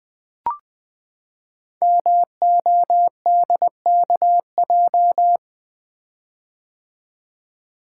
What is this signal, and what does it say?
Morse code, heard as a keyed tone, about 20 wpm, 700 Hz. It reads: MODKJ
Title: Code Groups - 5 Letter:Numbers - 20wpm